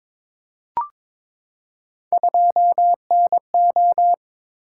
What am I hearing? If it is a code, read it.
Morse code, heard as a keyed tone, about 22 wpm, 700 Hz. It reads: 2NO
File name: practice / 22wpm / Three Letter-Number Combinations - 22wpm